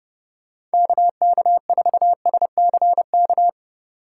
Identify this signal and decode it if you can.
Morse code, heard as a keyed tone, about 30 wpm, 700 Hz. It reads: KK4SCK
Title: USA Call Signs - 6 Characters - 2x Character Spacing - 30wpm